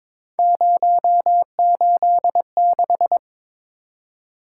Morse code, heard as a keyed tone, about 22 wpm, 700 Hz. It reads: 086
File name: practice / 22wpm / Three Letter-Number Combinations - 22wpm